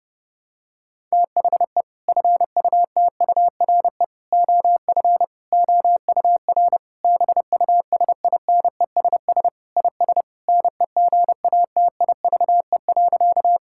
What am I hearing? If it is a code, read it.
Morse code, heard as a keyed tone, about 32 wpm, 700 Hz. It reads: THE FUTURE OF OUR BUSINESS IS NEGATIVE.
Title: Sentences from Top 1000 Words - Review Part 6 - 30wpm